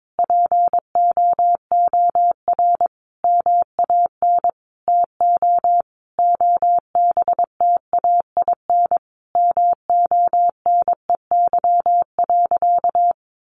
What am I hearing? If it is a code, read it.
Morse code, heard as a keyed tone, about 22 wpm, 700 Hz. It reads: POOR MAN TO OBTAIN MONEY.
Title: Sentences from Top 1300 Words 22wpm